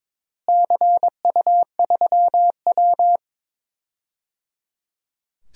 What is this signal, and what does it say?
Morse code, heard as a keyed tone, about 21 wpm, 700 Hz. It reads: CU3W
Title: DXCC-100 Call Signs - 4 Characters - Repeated 3x - 22wpm